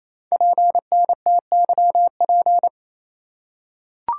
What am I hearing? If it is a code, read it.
Morse code, heard as a keyed tone, about 27 wpm, 700 Hz. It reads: PNTYP
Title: Code Groups - 5 Letter:Numbers - 28wpm